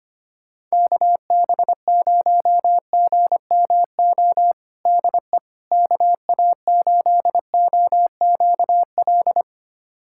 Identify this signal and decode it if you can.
Morse code, heard as a keyed tone, about 25 wpm, 700 Hz. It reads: KB0GMO DE KA8OQL
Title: USA Callsign DE Callsign 25wpm